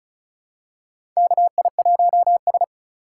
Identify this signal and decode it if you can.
Morse code, heard as a keyed tone, about 36 wpm, 700 Hz. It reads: KI1S
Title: USA Call Signs - 4 Characters - 3x Character Spacing - 35wpm